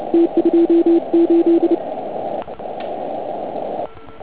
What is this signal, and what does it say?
Morse code, heard as a keyed tone, about 31 wpm, 340 Hz. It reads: T28
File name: Archiv / CUC149 / OK1IF CUC149